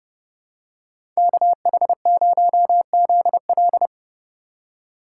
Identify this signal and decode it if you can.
Morse code, heard as a keyed tone, about 31 wpm, 700 Hz. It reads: KH0ZL
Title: USA Call Signs - 5 Characters - 3x Character Spacing - 30wpm